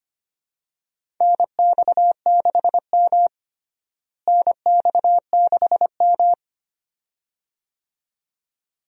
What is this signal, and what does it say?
Morse code, heard as a keyed tone, about 25 wpm, 700 Hz. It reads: NX6M NX6M
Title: USA Call Signs - 4 Characters - Repeated 2x - 25wpm